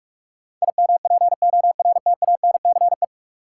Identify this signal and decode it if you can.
Morse code, heard as a keyed tone, about 46 wpm, 700 Hz. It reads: IMPORTANCE
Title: Sentences from Top 1200 Words - New Word Drill Alt - 45wpm